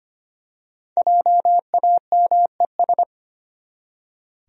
Morse code, heard as a keyed tone, about 25 wpm, 700 Hz. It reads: JAMES
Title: USA Common Names - 25wpm